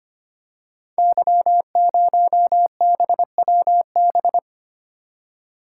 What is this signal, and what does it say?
Morse code, heard as a keyed tone, about 25 wpm, 700 Hz. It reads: Y0BWB
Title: Code Groups - 5 Letter:Numbers - 25wpm